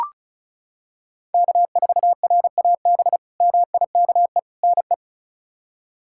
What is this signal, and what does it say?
Morse code, heard as a keyed tone, about 34 wpm, 700 Hz. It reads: K4RAB MIKE NE
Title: K1USN SST QSO Practice - 35wpm